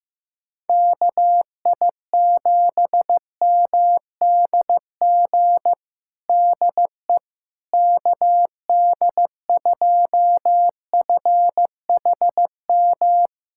Morse code, heard as a keyed tone, about 15 wpm, 700 Hz. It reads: KI7MDG DE KD2FHM
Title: USA Callsign DE Callsign 15wpm